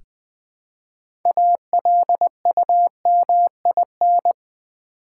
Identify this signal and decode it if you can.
Morse code, heard as a keyed tone, about 20 wpm, 700 Hz. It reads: ALUMIN
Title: Build A Word - 2 Letters At A Time - 8-Letter Words - 20wpm